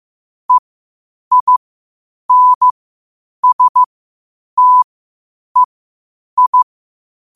Morse code, heard as a keyed tone, about 15 wpm, spaced as for 8 wpm, 1000 Hz. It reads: EINSTEI
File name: qotdfiles / 8wpm